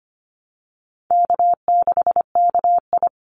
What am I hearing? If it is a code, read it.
Morse code, heard as a keyed tone, about 25 wpm, 700 Hz. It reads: K6KI
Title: Easy Call Signs Repeated 3x - 25wpm